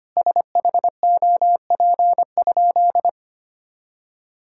Morse code, heard as a keyed tone, about 25 wpm, 700 Hz. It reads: SHOP?
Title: Sentences from Top 1600 Words 25wpm